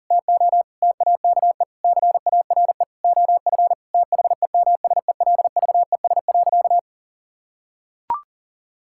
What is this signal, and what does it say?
Morse code, heard as a keyed tone, about 40 wpm, 700 Hz. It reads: TO TAKE CARE OF THEMSELVES.
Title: Sentences from Top 1000 Words - Review Part 5 - 40wpm